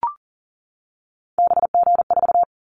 Morse code, heard as a keyed tone, about 40 wpm, 700 Hz. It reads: BG4
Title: Callsign Prefixes for Top 100 DXCC - 40wpm